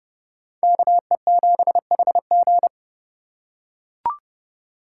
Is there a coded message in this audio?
Yes: Morse code from 0.6 to 2.7 s, heard as a keyed tone, about 29 wpm, 700 Hz. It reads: KE7HG